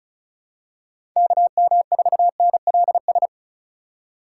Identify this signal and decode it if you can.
Morse code, heard as a keyed tone, about 36 wpm, 700 Hz. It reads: KM4NLS
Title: USA Call Signs - 6 Characters - 3x Character Spacing - 35wpm